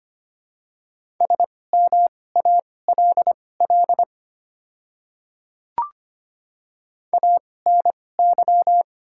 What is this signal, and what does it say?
Morse code, heard as a keyed tone, about 25 wpm, spaced as for 18 wpm, 690 Hz. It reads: SMALL ANY
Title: 2x Character Spacing - Top 200 Words - 25wpm